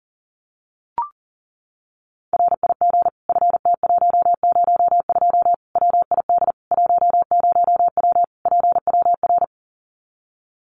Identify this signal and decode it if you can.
Morse code, heard as a keyed tone, about 40 wpm, 700 Hz. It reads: RIG FT102 WID 10W PWR
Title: Common QSO Phrases 40wpm